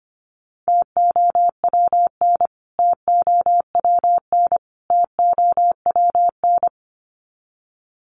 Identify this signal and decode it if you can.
Morse code, heard as a keyed tone, about 25 wpm, 700 Hz. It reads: TOWN TOWN TOWN
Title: Top 500 Words Repeated 25wpm